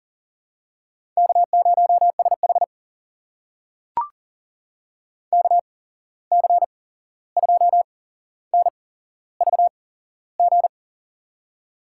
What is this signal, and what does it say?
Morse code, heard as a keyed tone, about 40 wpm, spaced as for 15 wpm, 700 Hz. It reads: K0SH KC2NVG
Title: USA Call Signs - 8x Character Spacing - 40wpm